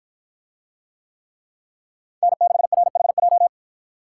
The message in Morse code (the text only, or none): N6RHJ